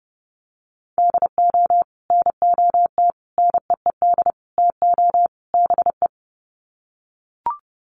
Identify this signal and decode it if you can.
Morse code, heard as a keyed tone, about 30 wpm, 700 Hz. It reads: DO NOT NEED TO BE
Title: Sets of 5 Words Encore 30wpm